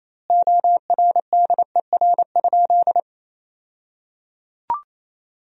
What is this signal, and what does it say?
Morse code, heard as a keyed tone, about 28 wpm, 700 Hz. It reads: ORDER?
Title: 2x Word-Spacing - Sentences from Top 400 Words - 28wpm